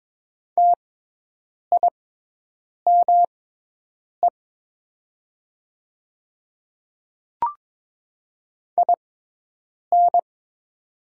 Morse code, heard as a keyed tone, about 22 wpm, spaced as for 8 wpm, 700 Hz. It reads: TIME IN
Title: 6x Character Spacing - Top 100 Words - 22wpm